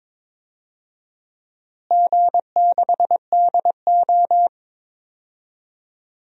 Morse code, heard as a keyed tone, about 21 wpm, 700 Hz. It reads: G6DO
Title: DXCC-100 Call Signs - 4 Characters - Repeated 2x - 22wpm